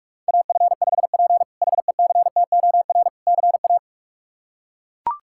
Morse code, heard as a keyed tone, about 42 wpm, 700 Hz. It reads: AF5P HECTOR CA
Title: K1USN SST QSO Practice - 45wpm